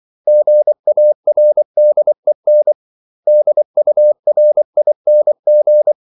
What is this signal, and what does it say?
Morse code, heard as a keyed tone, about 24 wpm, 600 Hz. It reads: GARDEN DURING
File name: CW / Practice / 500MostCommonEnglishWords24